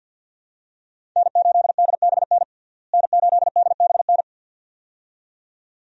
Morse code, heard as a keyed tone, about 46 wpm, 700 Hz. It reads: N8DBN N8DBN
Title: USA Call Signs - 5 Characters - Repeated 2x - 50wpm